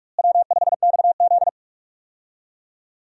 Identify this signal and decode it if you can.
Morse code, heard as a keyed tone, about 44 wpm, 700 Hz. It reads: W5XZ
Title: USA Call Signs - 4 Characters - Repeated 2x - 45wpm